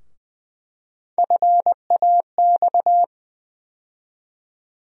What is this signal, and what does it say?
Morse code, heard as a keyed tone, about 20 wpm, 700 Hz. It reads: FAX
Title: USA Call Sign Postfixes - 3 Letters 20wpm